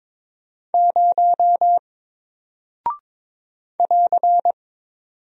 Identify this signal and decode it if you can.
Morse code, heard as a keyed tone, about 22 wpm, 700 Hz. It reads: 0 <AR>
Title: Single Character (Letter:Number:Punctuation:Prosign) Alt 22wpm